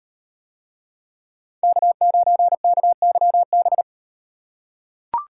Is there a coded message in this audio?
Yes: Morse code from 1.6 to 3.8 s, heard as a keyed tone, about 37 wpm, 700 Hz. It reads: K9KYB